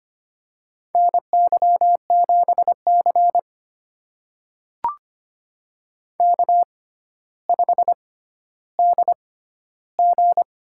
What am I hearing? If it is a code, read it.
Morse code, heard as a keyed tone, about 25 wpm, spaced as for 13 wpm, 700 Hz. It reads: NY7C K5DG
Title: USA Call Signs - 4 Characters - 6x Character Spacing - 25wpm